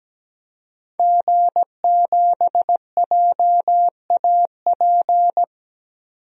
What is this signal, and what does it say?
Morse code, heard as a keyed tone, about 17 wpm, 700 Hz. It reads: G7JAP
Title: DXCC-100 Call Signs - 5 Characters - 17wpm